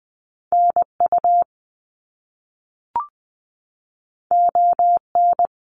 Morse code, heard as a keyed tone, about 20 wpm, 700 Hz. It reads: NU ON
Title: Canadian Provinces and Territories - Repeated 3x - 20wpm